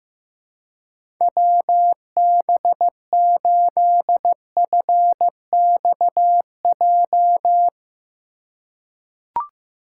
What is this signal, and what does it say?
Morse code, heard as a keyed tone, about 16 wpm, 700 Hz. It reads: WB8FXJ
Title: K1USN SST QSO Practice - 15wpm